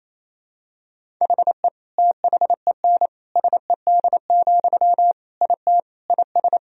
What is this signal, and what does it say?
Morse code, heard as a keyed tone, about 28 wpm, 700 Hz. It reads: HE THEN SED, IT IS
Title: Sentences from Top 300 Words and Abbreviations 28wpm